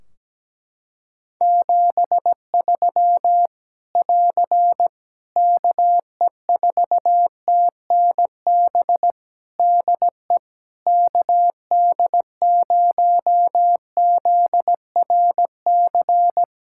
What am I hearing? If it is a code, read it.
Morse code, heard as a keyed tone, about 17 wpm, 700 Hz. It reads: 73 <AR> KE4TNB DE KD0ZRC
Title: K1USN SST QSO Practice 17wpm